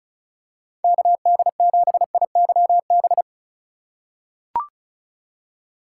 Morse code, heard as a keyed tone, about 36 wpm, 700 Hz. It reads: KD7IYB
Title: USA Call Signs - 6 Characters - 3x Character Spacing - 35wpm